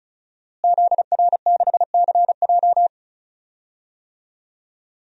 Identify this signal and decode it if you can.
Morse code, heard as a keyed tone, about 36 wpm, 700 Hz. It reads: ZR6CJ
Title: DXCC-100 Call Signs - 5 Characters - Repeated 2x - 35wpm